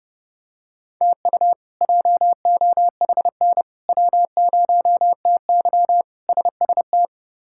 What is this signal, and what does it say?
Morse code, heard as a keyed tone, about 32 wpm, 700 Hz. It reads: TU JOHN W0TY SST
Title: K1USN SST QSO Practice - 30wpm